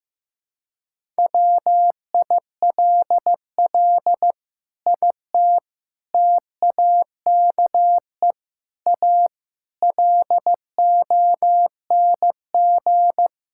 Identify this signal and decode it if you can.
Morse code, heard as a keyed tone, about 15 wpm, 700 Hz. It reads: WILL IT TAKE A LONG
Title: Sentences from Top 1000 Words - Review Part 6 - 15wpm